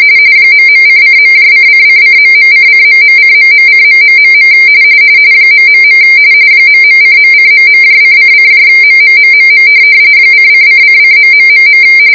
RTTY, about 45.45 baud. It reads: ..::.::::..::..:..:.... ... ..:.:..:::MMMMM :::::....